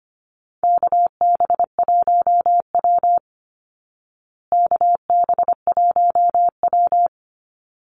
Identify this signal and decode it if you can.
Morse code, heard as a keyed tone, about 25 wpm, 700 Hz. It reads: KB1W KB1W
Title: Call Signs Repeated 3x - 25wpm